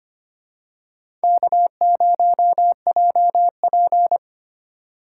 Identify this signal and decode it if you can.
Morse code, heard as a keyed tone, about 25 wpm, 700 Hz. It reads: K0JP
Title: USA Call Signs - 4 Characters - 6x Character Spacing - 25wpm